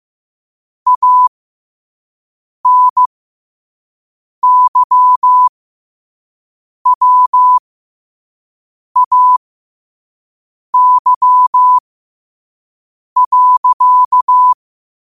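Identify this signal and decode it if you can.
Morse code, heard as a keyed tone, about 15 wpm, spaced as for 5 wpm, 1000 Hz. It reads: ANYWAY.
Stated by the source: Morse code sent at 5 wpm for learning.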